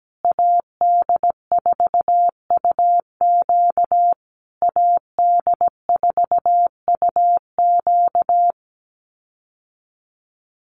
Morse code, heard as a keyed tone, about 17 wpm, 700 Hz. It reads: AD4UQ AD4UQ
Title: USA Call Signs Repeated 3x 17wpm